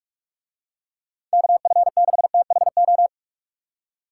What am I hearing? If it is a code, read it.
Morse code, heard as a keyed tone, about 45 wpm, 700 Hz. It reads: KF6THO